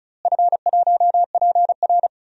Morse code, heard as a keyed tone, about 36 wpm, 700 Hz. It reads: F1PR
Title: DXCC-100 Call Signs - 4 Characters - Repeated 3x - 35wpm